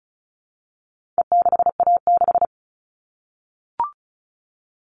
Morse code, heard as a keyed tone, about 34 wpm, 700 Hz. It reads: E6A6
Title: All Cut Numbers - 4-Digit 35wpm